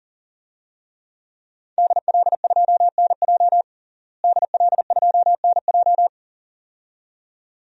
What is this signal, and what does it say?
Morse code, heard as a keyed tone, about 40 wpm, 700 Hz. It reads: DL2NJ DL2NJ
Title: DXCC-100 Call Signs - 5 Characters - Repeated 2x - 40wpm